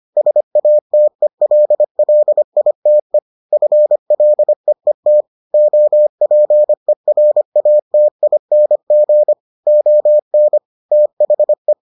Morse code, heard as a keyed tone, about 25 wpm, 600 Hz. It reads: SATELLITE FLEET OPERATING ON THE